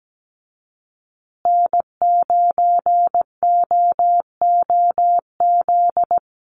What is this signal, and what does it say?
Morse code, heard as a keyed tone, about 17 wpm, 700 Hz. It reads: N9OOZ
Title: USA Call Signs Repeated 3x 17wpm